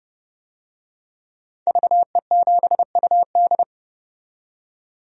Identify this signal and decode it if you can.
Morse code, heard as a keyed tone, about 30 wpm, 700 Hz. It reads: VE7UD
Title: DXCC-100 Call Signs - 5 Characters - 30wpm